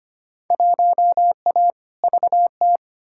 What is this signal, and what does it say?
Morse code, heard as a keyed tone, about 25 wpm, 700 Hz. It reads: 1A VT
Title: Field Day Exchanges Encore - 25wpm